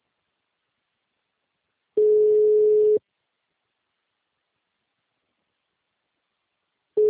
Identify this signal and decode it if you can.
Ringback tone (Europe, 425 Hz).